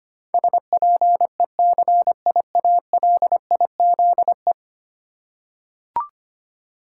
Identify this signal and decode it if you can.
Morse code, heard as a keyed tone, about 25 wpm, 690 Hz. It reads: SPECIALIZE
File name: 10 Letter Words - First 3 Letters Given - 25wpm